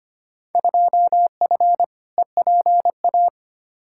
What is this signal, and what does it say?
Morse code, heard as a keyed tone, about 25 wpm, 700 Hz. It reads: 2F EPA
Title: Field Day Exchanges - 25wpm